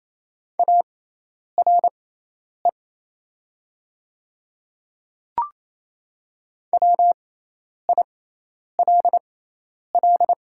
Morse code, heard as a keyed tone, about 28 wpm, spaced as for 10 wpm, 700 Hz. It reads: ARE WILL
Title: 6x Character Spacing - Top 300 Words - 28wpm